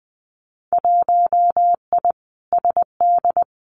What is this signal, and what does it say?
Morse code, heard as a keyed tone, about 19 wpm, 700 Hz. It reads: 1I SD
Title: Winter Field Day Exchanges 20wpm